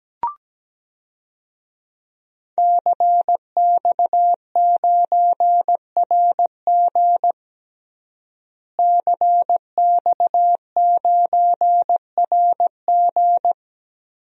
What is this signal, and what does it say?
Morse code, heard as a keyed tone, about 17 wpm, 700 Hz. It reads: CX9RG CX9RG
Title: DXCC-100 Call Signs - 5 Characters - Repeated 2x - 17wpm